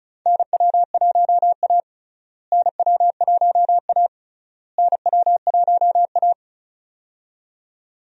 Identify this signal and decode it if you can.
Morse code, heard as a keyed tone, about 34 wpm, 700 Hz. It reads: NW1A NW1A NW1A
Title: USA Call Signs - 4 Characters - Repeated 3x - 35wpm